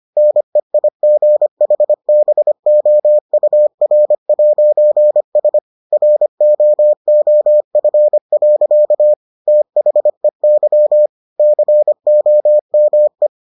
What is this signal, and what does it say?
Morse code, heard as a keyed tone, about 25 wpm, 600 Hz. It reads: NEIGHBOUR'S ROOF. THEY COME